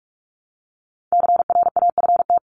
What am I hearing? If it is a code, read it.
Morse code, heard as a keyed tone, about 44 wpm, 700 Hz. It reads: CRAFT
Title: Top 751-1000 Words 45wpm